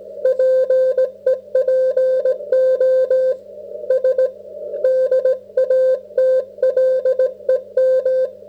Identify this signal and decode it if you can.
Morse code, heard as a keyed tone, about 17 wpm, 530 Hz. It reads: PEPO S DATLEM